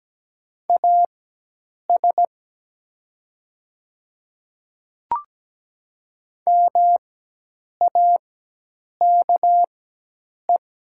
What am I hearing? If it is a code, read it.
Morse code, heard as a keyed tone, about 17 wpm, spaced as for 8 wpm, 700 Hz. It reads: AS MAKE
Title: 4x Character Spacing - Top 100 Words - 17wpm